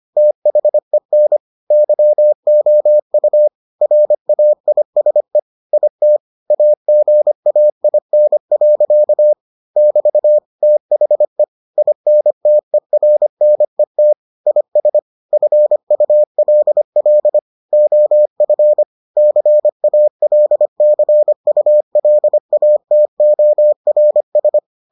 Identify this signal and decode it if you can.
Morse code, heard as a keyed tone, about 25 wpm, 600 Hz. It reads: THEN YOU RAISE IT AGAIN. <BT> THE INTERNET IS FULL OF CALCULATORS